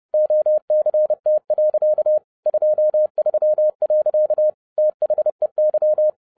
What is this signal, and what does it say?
Morse code, heard as a keyed tone, about 30 wpm, 610 Hz. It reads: OCT. 23. THEY